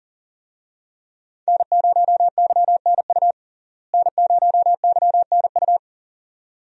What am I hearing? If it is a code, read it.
Morse code, heard as a keyed tone, about 40 wpm, 700 Hz. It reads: N0YNU N0YNU
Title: USA Call Signs - 5 Characters - Repeated 2x - 40wpm